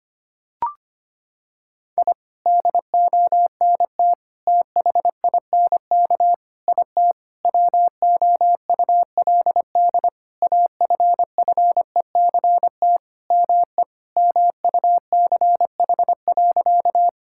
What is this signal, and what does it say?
Morse code, heard as a keyed tone, about 25 wpm, 700 Hz. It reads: I DONT THINK IT WOULD AFFECT ME MUCH.
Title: Sentences from Top 1000 Words - Review Part 6 - 25wpm